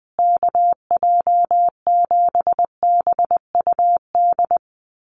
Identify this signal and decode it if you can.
Morse code, heard as a keyed tone, about 20 wpm, 700 Hz. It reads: KJ7BUD
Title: USA Call Signs Repeated 2x 20wpm